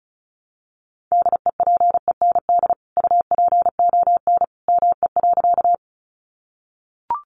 Morse code, heard as a keyed tone, about 34 wpm, 700 Hz. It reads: DEPEND UPON ME.
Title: Sentences from Top 1300 Words 35wpm